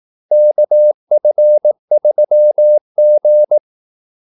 Morse code, heard as a keyed tone, about 18 wpm, 600 Hz. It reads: KF3G
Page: Kaldesignaler 18 wpm